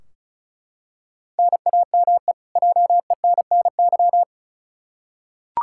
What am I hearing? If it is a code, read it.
Morse code, heard as a keyed tone, about 36 wpm, 700 Hz. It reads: NAME JENNY
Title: K1USN SST QSO Practice 35wpm